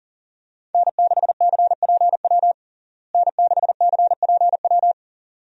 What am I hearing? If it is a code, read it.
Morse code, heard as a keyed tone, about 37 wpm, 700 Hz. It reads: N6CPW N6CPW
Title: USA Call Signs - 5 Characters - Repeated 2x - 40wpm